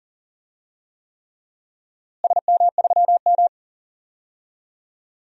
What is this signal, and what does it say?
Morse code, heard as a keyed tone, about 40 wpm, 700 Hz. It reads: SM3M
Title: DXCC-100 Call Signs - 4 Characters - Repeated 3x - 40wpm